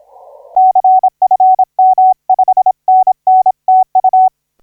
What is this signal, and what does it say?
Morse code, heard as a keyed tone, about 26 wpm, 750 Hz. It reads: CFM 5NN TU
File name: P29NO 1545UTC 20m